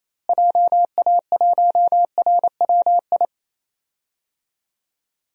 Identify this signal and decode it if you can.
Morse code, heard as a keyed tone, about 28 wpm, 700 Hz. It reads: JA1RWI
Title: USA Call Signs - 6 Characters - Repeated 2x - 28wpm